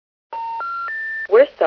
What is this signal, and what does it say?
Special information tone, with speech.